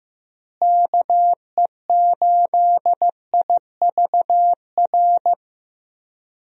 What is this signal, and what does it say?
Morse code, heard as a keyed tone, about 16 wpm, 700 Hz. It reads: KE8IVR
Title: USA Call Signs - 6 Characters - Repeated 2x - 15wpm